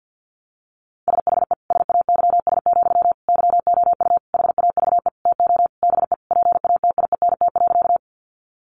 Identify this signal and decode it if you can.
Morse code, heard as a keyed tone, about 50 wpm, 700 Hz. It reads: SHE SAYS, YOU HAVE TO BE PATIENT.
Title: Easy Phrases 50wpm